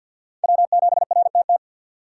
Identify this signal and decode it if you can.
Morse code, heard as a keyed tone, about 59 wpm, 700 Hz. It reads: W7RTT
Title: Speed Racing - USA 1x3 Call Signs 50wpm